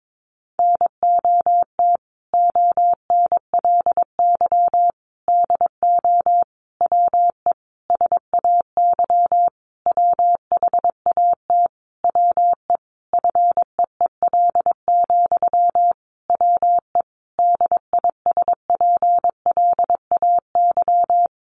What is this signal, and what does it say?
Morse code, heard as a keyed tone, about 22 wpm, 700 Hz. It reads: NOT ONLY DO WE SAY WHAT WE FEEL, WE DISPLAY
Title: Sentences from Top 1200 Words 22wpm